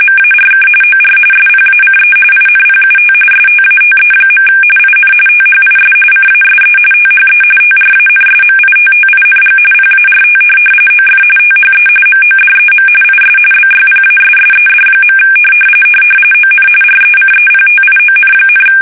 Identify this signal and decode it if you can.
RTTY, about 45.45 baud. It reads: MARK OR SPACE INPUT STATE. IN THIS CASE THE TRANSMITTING PART OF THE MODEM IS BYPASSED. ON RECEPTION,
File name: fsk850 / teletype-RTTY